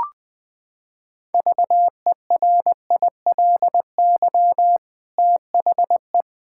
Morse code, heard as a keyed tone, about 20 wpm, 700 Hz. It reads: VERILY THE
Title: Pangram Challenge v2 - 20wpm